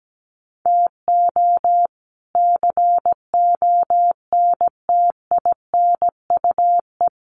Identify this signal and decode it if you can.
Morse code, heard as a keyed tone, about 17 wpm, 700 Hz. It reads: TO CONTINUE
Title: Sentences from Top 1500 Words 17wpm